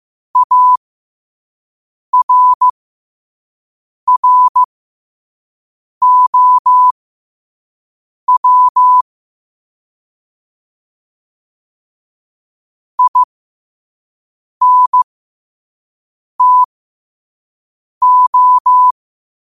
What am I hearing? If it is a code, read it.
Morse code, heard as a keyed tone, about 15 wpm, spaced as for 5 wpm, 1000 Hz. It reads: ARROW INTO
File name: qotdfiles / 5wpm